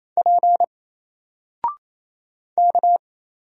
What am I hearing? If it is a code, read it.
Morse code, heard as a keyed tone, about 27 wpm, 700 Hz. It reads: P K
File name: Single-Letter-Number-28wpm